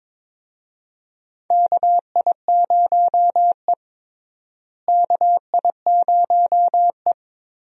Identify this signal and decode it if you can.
Morse code, heard as a keyed tone, about 21 wpm, 700 Hz. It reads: KI0E KI0E
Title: USA Call Signs - 4 Characters - Repeated 3x - 22wpm